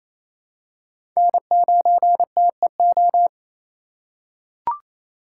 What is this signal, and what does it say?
Morse code, heard as a keyed tone, about 28 wpm, 700 Hz. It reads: N9TEO
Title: USA Call Signs - 5 Characters - 4x Character Spacing - 28wpm